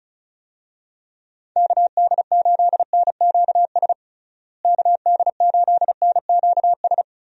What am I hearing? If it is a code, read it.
Morse code, heard as a keyed tone, about 36 wpm, 700 Hz. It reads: KD8NQS KD8NQS
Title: USA Call Signs - 6 Characters - Repeated 3x - 35wpm